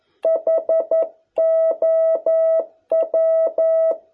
Morse code, heard as a keyed tone, about 11 wpm, 650 Hz. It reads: HOW